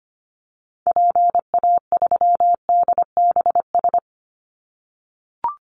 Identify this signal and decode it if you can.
Morse code, heard as a keyed tone, about 25 wpm, 700 Hz. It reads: PA3DBS
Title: Call Signs - 25wpm